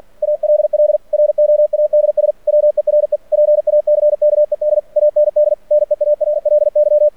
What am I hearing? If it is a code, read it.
Morse code, heard as a keyed tone, about 49 wpm, 600 Hz. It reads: ALL ROADS WERE PACKED AND NEARLY
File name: archived / CRSnet